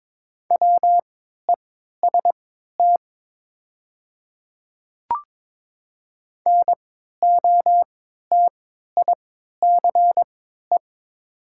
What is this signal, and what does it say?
Morse code, heard as a keyed tone, about 22 wpm, spaced as for 13 wpm, 700 Hz. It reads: WEST NOTICE
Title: 3x Character Spacing - Top 500 Words - 22wpm